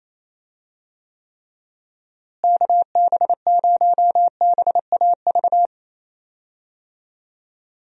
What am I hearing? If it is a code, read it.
Morse code, heard as a keyed tone, about 27 wpm, 700 Hz. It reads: KB0BAV
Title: USA Call Signs - 6 Characters - Repeated 2x - 28wpm